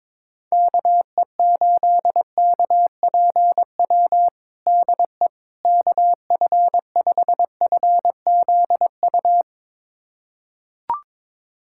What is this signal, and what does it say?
Morse code, heard as a keyed tone, about 22 wpm, 700 Hz. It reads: KE8KPW DE KF5FZU
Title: USA Callsign DE Callsign 22wpm